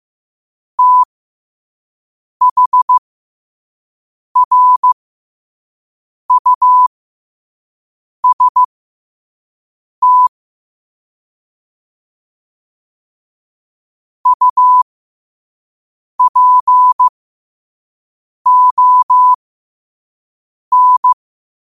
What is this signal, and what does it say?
Morse code, heard as a keyed tone, about 15 wpm, spaced as for 5 wpm, 1000 Hz. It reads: THRUST UPON
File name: qotdfiles / 5wpm